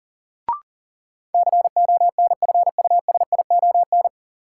Morse code, heard as a keyed tone, about 40 wpm, 700 Hz. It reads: CONFUSION
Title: Sentences from Top 1600 Words - New Word Drill Alt - 40wpm